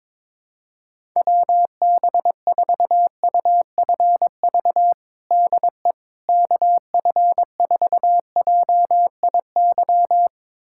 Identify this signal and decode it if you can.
Morse code, heard as a keyed tone, about 21 wpm, 700 Hz. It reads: WB4UFV DE KF4JIY
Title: USA Callsign DE Callsign - 6 Characters - 22wpm